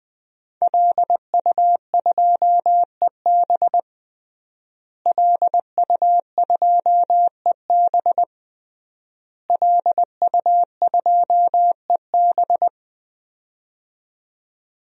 Morse code, heard as a keyed tone, about 21 wpm, 700 Hz. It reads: LU2EB LU2EB LU2EB
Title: DXCC-100 Call Signs - 5 Characters - Repeated 3x - 20wpm